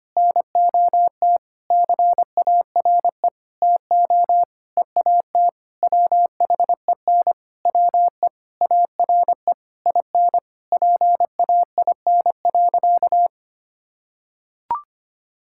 Morse code, heard as a keyed tone, about 25 wpm, 700 Hz. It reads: NOT CARE TO EAT WHEN WE ARE IN PAIN.
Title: Sentences from Top 1600 Words 25wpm